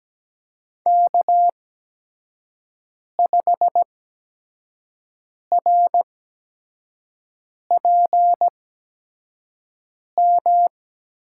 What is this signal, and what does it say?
Morse code, heard as a keyed tone, about 17 wpm, spaced as for 5 wpm, 700 Hz. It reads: K5RPM
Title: USA Call Signs - 5 Characters - 8x Character Spacing - 17wpm